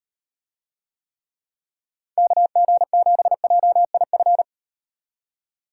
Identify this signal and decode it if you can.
Morse code, heard as a keyed tone, about 38 wpm, 700 Hz. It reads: KG7JIF